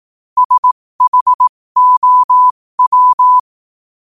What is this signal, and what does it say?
Morse code, heard as a keyed tone, about 18 wpm, spaced as for 15 wpm, 1000 Hz. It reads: SHOW
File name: qotdfiles / 15wpm